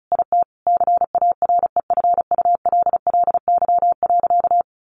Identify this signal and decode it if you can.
Morse code, heard as a keyed tone, about 34 wpm, 700 Hz. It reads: IT CAREFULLY.
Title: Sentences from Top 1300 Words 35wpm